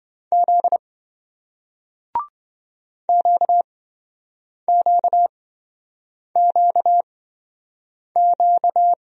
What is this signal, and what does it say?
Morse code, heard as a keyed tone, about 27 wpm, spaced as for 7 wpm, 700 Hz. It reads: Z QQQQ